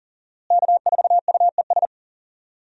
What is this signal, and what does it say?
Morse code, heard as a keyed tone, about 40 wpm, 700 Hz. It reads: K4UES
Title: USA Call Signs - 5 Characters - Repeated 3x - 40wpm